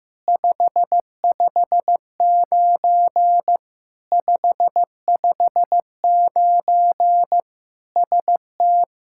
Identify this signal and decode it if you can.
Morse code, heard as a keyed tone, about 15 wpm, 700 Hz. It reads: 559 559 ST
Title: Practice QSO - Exchange 1 - Beginning - 15wpm